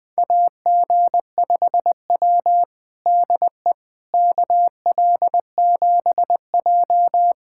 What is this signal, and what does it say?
Morse code, heard as a keyed tone, about 21 wpm, 700 Hz. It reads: AG5W DE KL7J
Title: USA Callsign DE Callsign - 4 Characters - 20wpm